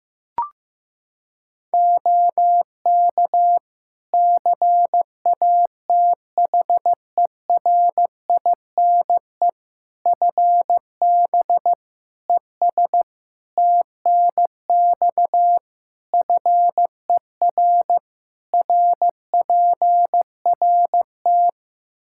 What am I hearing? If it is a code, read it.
Morse code, heard as a keyed tone, about 15 wpm, 700 Hz. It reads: OK CATHERINE FB ES TNX FER RPRT
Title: Practice QSO - Exchange 2 - Info - 15wpm